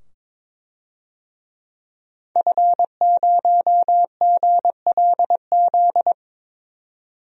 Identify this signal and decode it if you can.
Morse code, heard as a keyed tone, about 22 wpm, 700 Hz. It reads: F0GLZ
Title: DXCC-100 Call Signs - 5 Characters - 22wpm